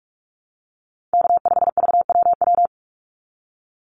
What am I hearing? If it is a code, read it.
Morse code, heard as a keyed tone, about 44 wpm, 700 Hz. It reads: K5VWW
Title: Call Signs Repeated 2x - 45wpm